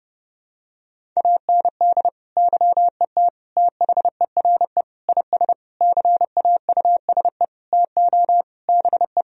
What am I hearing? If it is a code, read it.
Morse code, heard as a keyed tone, about 32 wpm, 700 Hz. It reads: AND YET THERE IS CAUSE TO BE
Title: Sentences from Top 1000 Words - Review Part 1 - 30wpm